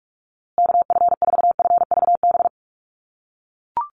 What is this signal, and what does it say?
Morse code, heard as a keyed tone, about 44 wpm, 700 Hz. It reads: KF4FVB
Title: USA Call Signs Repeated 3x 45wpm